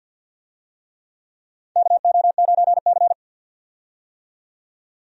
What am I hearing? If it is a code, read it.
Morse code, heard as a keyed tone, about 51 wpm, 700 Hz. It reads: KO9C
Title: USA Call Signs - 4 Characters - 50wpm